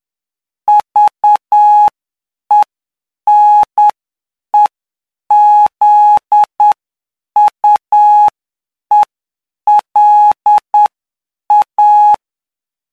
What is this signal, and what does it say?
Morse code, heard as a keyed tone, about 9 wpm, spaced as for 7 wpm, 840 Hz. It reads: VENEZUELA
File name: cw / py2lcd / 09-PY2LCD-Li��o 18